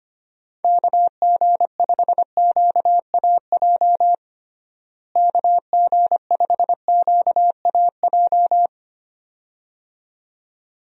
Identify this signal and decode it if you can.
Morse code, heard as a keyed tone, about 25 wpm, 700 Hz. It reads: KG5QAJ KG5QAJ
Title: USA Call Signs - 6 Characters - Repeated 2x - 25wpm